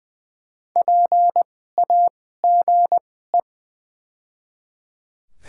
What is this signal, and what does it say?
Morse code, heard as a keyed tone, about 20 wpm, spaced as for 14 wpm, 700 Hz. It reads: PAGE
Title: 2x Character Spacing - Top 300 Words - 20wpm